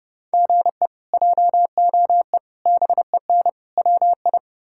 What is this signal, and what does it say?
Morse code, heard as a keyed tone, about 30 wpm, 700 Hz. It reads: GE JOE BEN WI
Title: K1USN SST QSO Practice - 30wpm